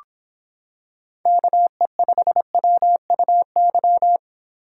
Morse code, heard as a keyed tone, about 25 wpm, 700 Hz. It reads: KE5WUY